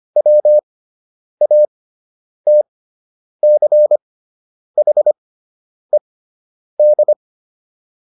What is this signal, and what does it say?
Morse code, heard as a keyed tone, about 25 wpm, spaced as for 9 wpm, 600 Hz. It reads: WATCHED